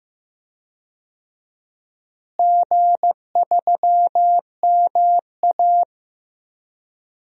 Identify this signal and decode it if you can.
Morse code, heard as a keyed tone, about 15 wpm, 700 Hz. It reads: G3MA
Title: DXCC-100 Call Signs - 4 Characters - Repeated 2x - 15wpm